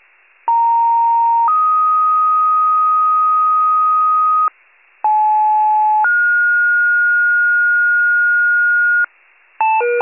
Two-tone page, heard as a one-second tone, then a three-second tone, about 930 then 1250 Hz, about 860 then 1430 Hz, with noise. It is heard from 0.5 to 9.0 s.
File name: pager / 82 FIRE / wcalert